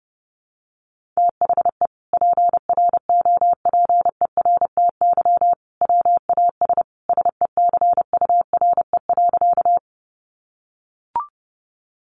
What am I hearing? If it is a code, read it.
Morse code, heard as a keyed tone, about 30 wpm, 700 Hz. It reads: THE PROPERTY WAS SECURE.
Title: Sentences from Top 1200 Words 30wpm